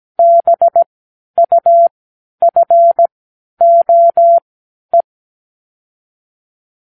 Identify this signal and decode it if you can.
Morse code, heard as a keyed tone, about 17 wpm, spaced as for 10 wpm, 680 Hz. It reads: BUFOE
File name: k7qo / Lesson 36 (A-X)